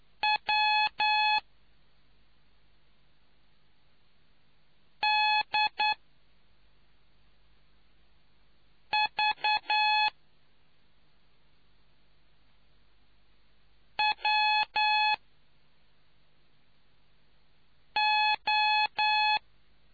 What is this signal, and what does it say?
Morse code, heard as a keyed tone, about 9 wpm, spaced as for 2 wpm, 850 Hz. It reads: WDVWO